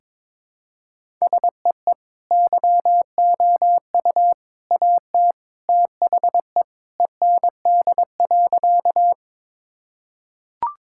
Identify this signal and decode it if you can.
Morse code, heard as a keyed tone, about 22 wpm, 700 Hz. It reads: SEE YOU AT THE END.
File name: practice / 22wpm / Sentences with 3-Word Phrases - 22wpm